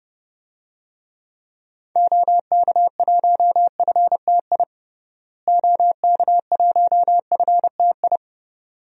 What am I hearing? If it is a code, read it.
Morse code, heard as a keyed tone, about 32 wpm, 700 Hz. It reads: OK1FTI OK1FTI
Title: DXCC-100 Call Signs - 6 Characters - Repeated 2x - 30wpm